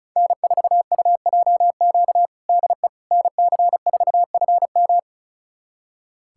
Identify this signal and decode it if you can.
Morse code, heard as a keyed tone, about 34 wpm, 700 Hz. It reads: N4UJQ DE NC4FM
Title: USA Callsign DE Callsign - 5 Characters - 35wpm